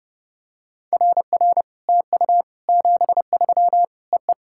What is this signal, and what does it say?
Morse code, heard as a keyed tone, about 30 wpm, 700 Hz. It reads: RR TU 73 EE
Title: SOTA QSO Practice 30wpm